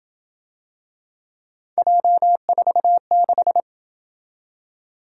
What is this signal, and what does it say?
Morse code, heard as a keyed tone, about 28 wpm, 700 Hz. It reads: J46